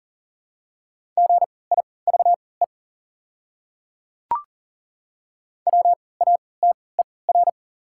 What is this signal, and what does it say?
Morse code, heard as a keyed tone, about 40 wpm, spaced as for 23 wpm, 700 Hz. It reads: GIVE WATER
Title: 3x Character Spacing - Top 300 Words - 40wpm